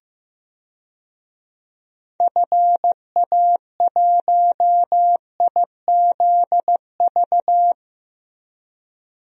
Morse code, heard as a keyed tone, about 15 wpm, 700 Hz. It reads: FA1IZV
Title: DXCC-100 Call Signs - 6 Characters - Repeated 3x - 15wpm